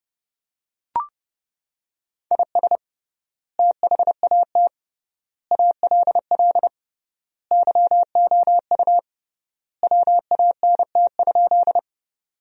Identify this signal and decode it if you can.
Morse code, heard as a keyed tone, about 30 wpm, 700 Hz. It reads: IS THAT ALL YOU WANT?
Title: 3x Word-Spacing - Sentences from Top 200 Words - 30wpm